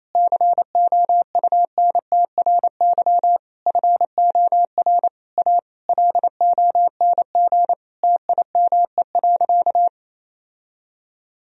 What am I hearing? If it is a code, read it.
Morse code, heard as a keyed tone, about 28 wpm, 700 Hz. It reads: COUNTRY FOR A LONG TIME.
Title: Sentences from Top 1000 Words - Review Part 1 - 28wpm